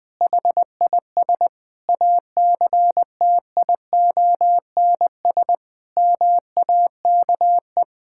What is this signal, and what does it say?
Morse code, heard as a keyed tone, about 21 wpm, 700 Hz. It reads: HIS ACTIONS MAKE
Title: Sentences with 5-Word Phrases - 20wpm